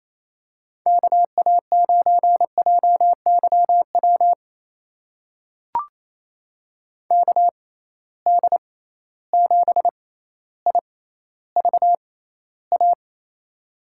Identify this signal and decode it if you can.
Morse code, heard as a keyed tone, about 28 wpm, spaced as for 14 wpm, 700 Hz. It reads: KA9JYW KD7IVA
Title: USA Call Signs - 6x Character Spacing - 28wpm